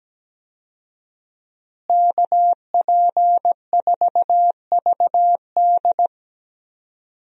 Morse code, heard as a keyed tone, about 17 wpm, 700 Hz. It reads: KP4VD
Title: DXCC-100 Call Signs - 5 Characters - Repeated 3x - 17wpm